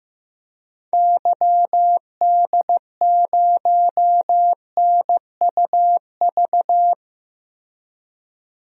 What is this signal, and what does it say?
Morse code, heard as a keyed tone, about 15 wpm, 700 Hz. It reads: YD0NUV
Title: DXCC-100 Call Signs - 6 Characters - 15wpm